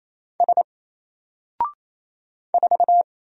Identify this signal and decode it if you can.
Morse code, heard as a keyed tone, about 27 wpm, 700 Hz. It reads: S 4
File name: Single-Letter-Number-28wpm